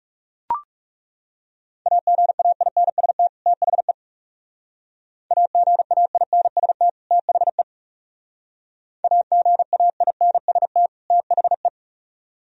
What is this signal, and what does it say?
Morse code, heard as a keyed tone, about 37 wpm, 700 Hz. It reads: AGAINST THE AGAINST THE AGAINST THE
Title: Speed-Racing - Sets of 2 Words 30wpm